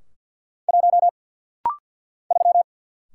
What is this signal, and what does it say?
Morse code, heard as a keyed tone, about 50 wpm, 700 Hz. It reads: 1 3
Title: Single Letter-Number - Rapid-Fire 50wpm